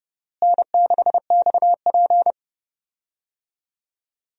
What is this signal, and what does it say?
Morse code, heard as a keyed tone, about 32 wpm, 700 Hz. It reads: N6XP
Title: USA Call Signs - 4 Characters - Repeated 2x - 30wpm